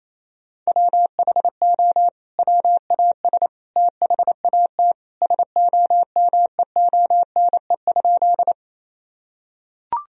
Morse code, heard as a keyed tone, about 28 wpm, 700 Hz. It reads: WHO WAS THAT SOMEONE?
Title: Sentences from Top 1000 Words - Review Part 4 - 28wpm